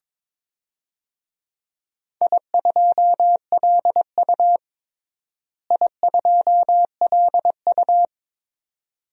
Morse code, heard as a keyed tone, about 23 wpm, 700 Hz. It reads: I2LU I2LU
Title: DXCC-100 Call Signs - 4 Characters - Repeated 2x - 22wpm